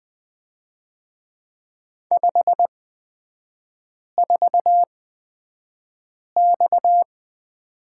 Morse code, heard as a keyed tone, about 20 wpm, spaced as for 5 wpm, 690 Hz. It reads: 54X
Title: 229 Lesson 37e - Letter x - Three Characters 20wpm